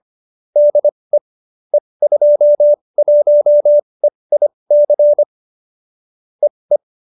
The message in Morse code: DE E21EIC EE

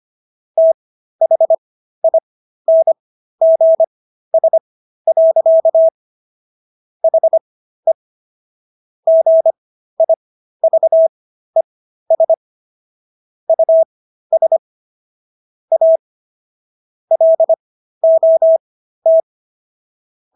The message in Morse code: THINGS. HE GIVES US A LOT

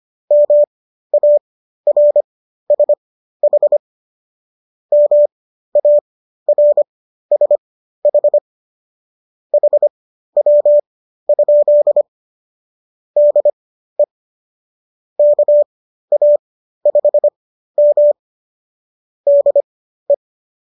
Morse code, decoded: MARSH MARSH HW? DE KA5M DE

